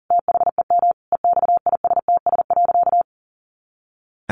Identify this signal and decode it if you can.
Morse code, heard as a keyed tone, about 40 wpm, 700 Hz. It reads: THEM EXISTS.
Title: A Sound Of Thunder 40wpm